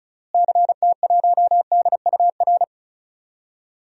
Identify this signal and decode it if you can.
Morse code, heard as a keyed tone, about 36 wpm, 700 Hz. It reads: CT1DUR
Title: DXCC-100 Call Signs - 6 Characters - Repeated 3x - 35wpm